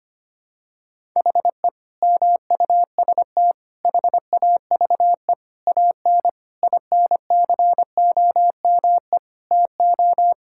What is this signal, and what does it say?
Morse code, heard as a keyed tone, about 25 wpm, 700 Hz. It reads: HE MUST HAVE AN INCOME TO
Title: Sentences from Top 1000 Words - Review Part 5 - 25wpm